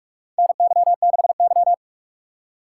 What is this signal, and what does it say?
Morse code, heard as a keyed tone, about 46 wpm, 700 Hz. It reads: NY6Y
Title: K1USN SST QSO Practice - 45wpm